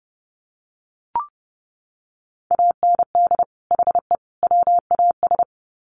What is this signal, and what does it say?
Morse code, heard as a keyed tone, about 30 wpm, 700 Hz. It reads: AND HE WAS
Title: Sets of 3 Words 30wpm